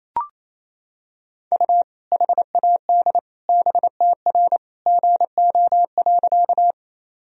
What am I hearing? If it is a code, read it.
Morse code, heard as a keyed tone, about 28 wpm, 700 Hz. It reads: U HAD BTR GO.
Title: Sentences from Top 400 Words and Abbreviations 28wpm